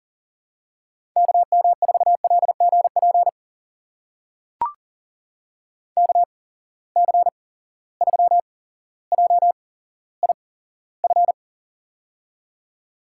Morse code, heard as a keyed tone, about 40 wpm, spaced as for 17 wpm, 700 Hz. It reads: KM4LGP KC3JIF
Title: USA Call Signs - 6 Characters - 8x Character Spacing - 40wpm